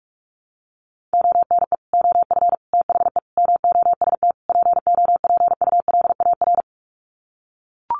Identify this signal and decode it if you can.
Morse code, heard as a keyed tone, about 44 wpm, 700 Hz. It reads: ONE OF THE MOST POPULAR
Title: Sets of 5 Words 45wpm